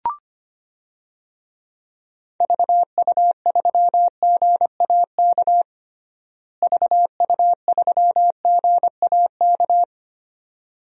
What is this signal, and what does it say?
Morse code, heard as a keyed tone, about 25 wpm, 700 Hz. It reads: VU3GAK VU3GAK
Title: DXCC-100 Call Signs - 6 Characters - Repeated 2x - 25wpm